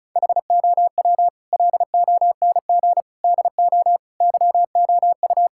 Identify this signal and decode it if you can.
Morse code, heard as a keyed tone, about 34 wpm, 700 Hz. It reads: HOW LONG DO YOU
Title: Sentences from Top 1600 Words 35wpm